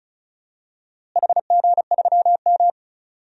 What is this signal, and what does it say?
Morse code, heard as a keyed tone, about 36 wpm, 700 Hz. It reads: HG3M
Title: DXCC-100 Call Signs - 4 Characters - Repeated 3x - 35wpm